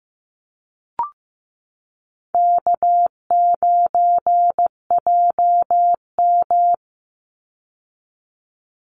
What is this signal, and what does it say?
Morse code, heard as a keyed tone, about 15 wpm, 700 Hz. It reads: K9JM
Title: Easy Call Signs Repeated 2x - 15wpm